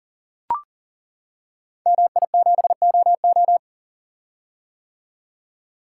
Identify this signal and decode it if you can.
Morse code, heard as a keyed tone, about 42 wpm, 700 Hz. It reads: MI7OO
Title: Code Groups - 5 Letter:Numbers - 40wpm